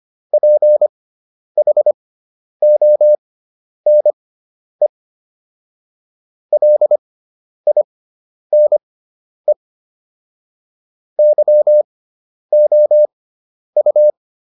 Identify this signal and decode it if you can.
Morse code, heard as a keyed tone, about 25 wpm, spaced as for 10 wpm, 600 Hz. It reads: PHONE LINE YOU